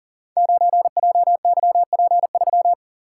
Morse code, heard as a keyed tone, about 39 wpm, 700 Hz. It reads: 9JYP3
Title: Code Groups - 5 Letter:Numbers - 40wpm